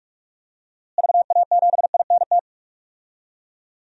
Morse code, heard as a keyed tone, about 41 wpm, 700 Hz. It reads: VA7INT
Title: DXCC-100 Call Signs - 6 Characters - Repeated 3x - 45wpm